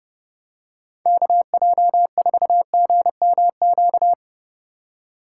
Morse code, heard as a keyed tone, about 30 wpm, 700 Hz. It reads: KJ4GMQ